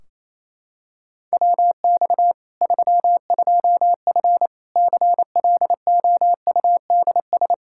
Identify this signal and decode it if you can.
Morse code, heard as a keyed tone, about 28 wpm, 700 Hz. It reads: WX 32F CLOUDS
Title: Practice QSO - Exchange 2 - Info - 28wpm